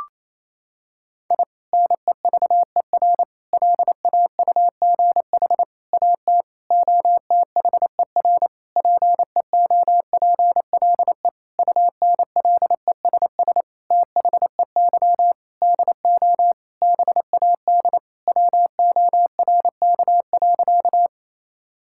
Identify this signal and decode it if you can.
Morse code, heard as a keyed tone, about 28 wpm, 700 Hz. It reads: I NEVER LAUGH AT OTHER PEOPLE UNLESS THEY DO BAD WORK.
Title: Sentences from Top 1000 Words - Review Part 6 - 28wpm